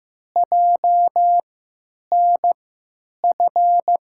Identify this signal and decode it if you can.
Morse code, heard as a keyed tone, about 15 wpm, spaced as for 9 wpm, 700 Hz. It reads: JNF
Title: USA Call Signs - 6 Characters - 3x Character Spacing - 15wpm